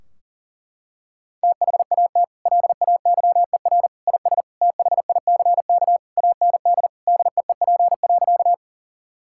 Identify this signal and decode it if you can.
Morse code, heard as a keyed tone, about 37 wpm, 700 Hz. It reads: THAT LAYER IS THICK AND DEEP.
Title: Sentences from Top 1600 Words 40wpm